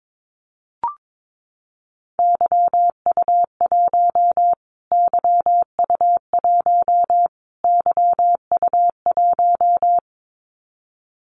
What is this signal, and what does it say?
Morse code, heard as a keyed tone, about 22 wpm, 700 Hz. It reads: YU1 YU1 YU1
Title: Callsign Prefixes for Top 25 DXCC - Repeated 3x - 22wpm